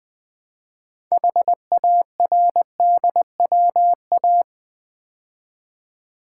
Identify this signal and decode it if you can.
Morse code, heard as a keyed tone, about 20 wpm, 700 Hz. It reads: HARDWA
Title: Build A Word - 3 Letters At A Time - 8-Letter Words - 20wpm